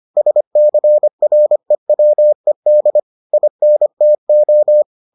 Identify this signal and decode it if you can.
Morse code, heard as a keyed tone, about 25 wpm, 600 Hz. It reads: SCREWED INTO